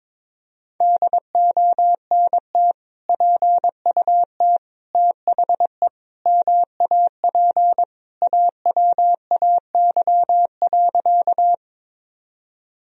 Morse code, heard as a keyed tone, about 22 wpm, 700 Hz. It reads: DONT PUT THE MAP AWAY.
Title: Sentences from Top 1000 Words - Review Part 2 - 22wpm